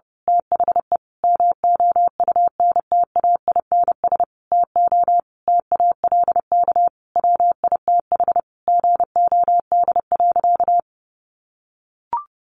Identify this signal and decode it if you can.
Morse code, heard as a keyed tone, about 30 wpm, 700 Hz. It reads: THE MOUNTAINS TO TALK WITH GOD.
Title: A Sound Of Thunder 30wpm